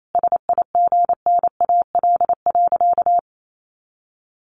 Morse code, heard as a keyed tone, about 28 wpm, 700 Hz. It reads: SIGNAL.
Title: Sentences from Top 1400 Words 28wpm